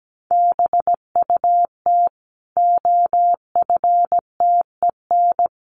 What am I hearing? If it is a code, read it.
Morse code, heard as a keyed tone, about 17 wpm, 700 Hz. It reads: BUT OFTEN
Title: Sentences from Top 1400 Words 17wpm